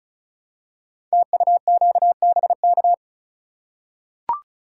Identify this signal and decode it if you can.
Morse code, heard as a keyed tone, about 36 wpm, 700 Hz. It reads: TUQBK
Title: Code Groups - 5 Letters - 35wpm